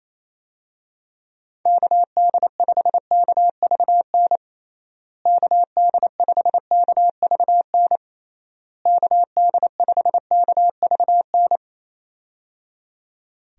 Morse code, heard as a keyed tone, about 27 wpm, 700 Hz. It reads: KD5KVN KD5KVN KD5KVN
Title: USA Call Signs - 6 Characters - Repeated 3x - 28wpm